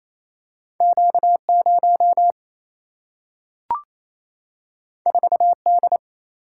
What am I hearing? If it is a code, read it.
Morse code, heard as a keyed tone, about 28 wpm, 700 Hz. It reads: Q 0 4 D
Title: Single-Letter Single-Number 28wpm